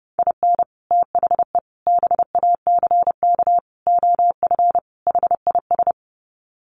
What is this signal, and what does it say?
Morse code, heard as a keyed tone, about 30 wpm, 700 Hz. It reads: IN THE BACK OF HIS
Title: Sets of 5 Words Encore2 30wpm